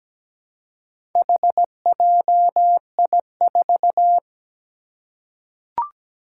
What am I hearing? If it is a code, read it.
Morse code, heard as a keyed tone, about 17 wpm, 700 Hz. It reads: HJI4
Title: Four Challenging Letter-Number Combinations 17wpm